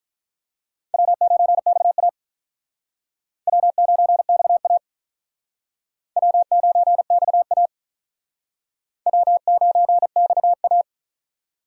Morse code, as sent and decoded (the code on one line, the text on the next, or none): .-- ----. -..- .- / .-- ----. -..- .- / .-- ----. -..- .- / .-- ----. -..- .-
W9XA W9XA W9XA W9XA